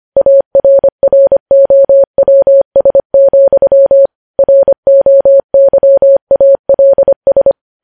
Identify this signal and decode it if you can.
Morse code, heard as a keyed tone, about 25 wpm, 550 Hz. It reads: ARROWS, ROYALS
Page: news-headlines-in-morse-code-at-25-wpm